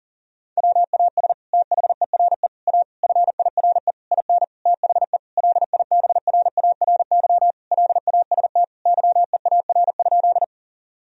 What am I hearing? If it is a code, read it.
Morse code, heard as a keyed tone, about 40 wpm, 700 Hz. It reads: WAS THERE A FIRE IN THE LIBRARY LAST YEAR?
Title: Sentences from Top 1000 Words - Review Part 4 - 40wpm